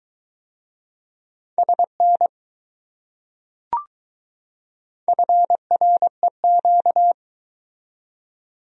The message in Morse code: SN FREQ

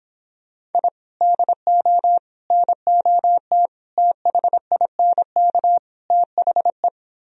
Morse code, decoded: I DO NOT THINK THE

